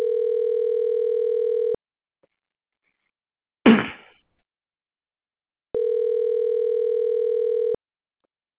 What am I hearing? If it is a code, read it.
Ringback tone (North America), 440 + 480 Hz.